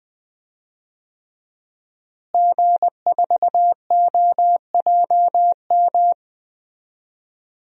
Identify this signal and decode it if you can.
Morse code, heard as a keyed tone, about 20 wpm, 700 Hz. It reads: G4OJM
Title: DXCC-100 Call Signs - 5 Characters - Repeated 3x - 20wpm